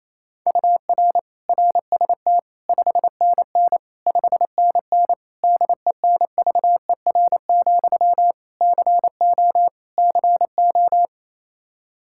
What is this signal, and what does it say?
Morse code, heard as a keyed tone, about 28 wpm, 700 Hz. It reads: UR RST 5NN 5NN DENVER, CO CO
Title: Practice QSO - Exchange 1 - Beginning - 28wpm